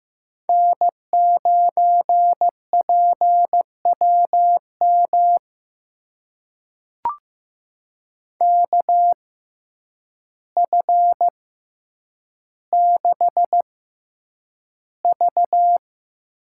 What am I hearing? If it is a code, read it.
Morse code, heard as a keyed tone, about 16 wpm, spaced as for 8 wpm, 700 Hz. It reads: N9PWM KF6V
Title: USA Call Signs - 6x Character Spacing - 15wpm